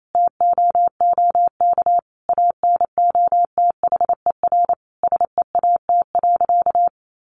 Morse code, heard as a keyed tone, about 28 wpm, 700 Hz. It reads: TOOK ANOTHER SEAT.
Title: Sentences from Top 1500 Words 28wpm